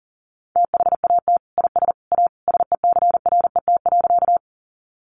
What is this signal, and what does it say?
Morse code, heard as a keyed tone, about 40 wpm, 700 Hz. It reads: THAT IS A SECRET.
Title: Sentences from Top 1200 Words 40wpm